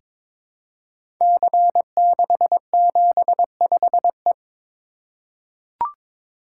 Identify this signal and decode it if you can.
Morse code, heard as a keyed tone, about 22 wpm, 700 Hz. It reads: C675E
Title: Code Groups - 5 Letter:Numbers - 22wpm